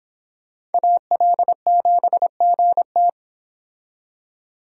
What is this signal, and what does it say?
Morse code, heard as a keyed tone, about 32 wpm, 700 Hz. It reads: AL7GT